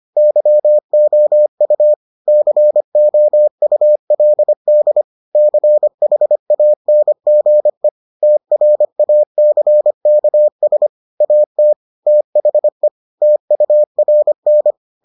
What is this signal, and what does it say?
Morse code, heard as a keyed tone, about 25 wpm, 600 Hz. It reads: YOU COULD CHANGE TRACKS AT THE TURN